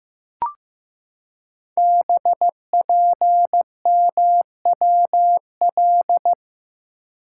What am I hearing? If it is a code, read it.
Morse code, heard as a keyed tone, about 15 wpm, 700 Hz. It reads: BPMWL
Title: Code Groups - 5 Letters - 15wpm